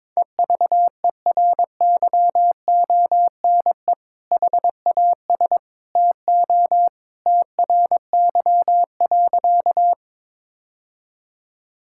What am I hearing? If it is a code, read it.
Morse code, heard as a keyed tone, about 22 wpm, 700 Hz. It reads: EVERYONE HAS TO TRY.
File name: practice / 22wpm / Sentences with 2-Word Phrases - 22wpm